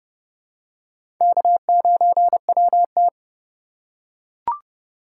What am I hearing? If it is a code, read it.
Morse code, heard as a keyed tone, about 29 wpm, 700 Hz. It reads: K9WT
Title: USA Call Signs - 4 Characters - 2x Character Spacing - 30wpm